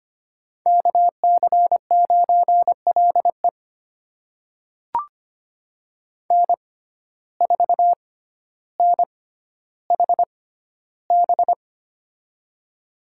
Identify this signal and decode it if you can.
Morse code, heard as a keyed tone, about 25 wpm, spaced as for 13 wpm, 700 Hz. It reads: KC9LE N4NHB
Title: USA Call Signs - 6x Character Spacing - 25wpm